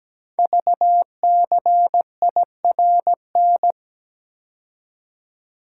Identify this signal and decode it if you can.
Morse code, heard as a keyed tone, about 17 wpm, 700 Hz. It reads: VCIRN
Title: Code Groups - 5 Letter:Numbers - 17wpm